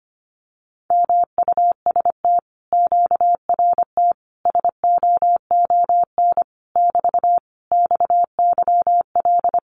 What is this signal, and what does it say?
Morse code, heard as a keyed tone, about 25 wpm, 700 Hz. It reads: MUST QRT SOON <BT> XYL
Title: Common QSO Phrases Encore 25wpm